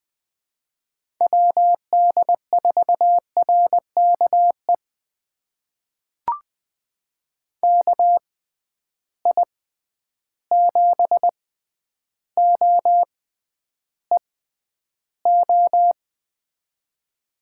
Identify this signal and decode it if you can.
Morse code, heard as a keyed tone, about 20 wpm, spaced as for 10 wpm, 700 Hz. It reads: WD4RKE KI7OEO
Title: USA Call Signs - 6 Characters - 6x Character Spacing - 20wpm